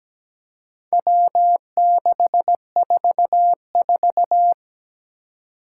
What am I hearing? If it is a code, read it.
Morse code, heard as a keyed tone, about 17 wpm, 700 Hz. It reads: W644
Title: Four Challenging Letter-Number Combinations 17wpm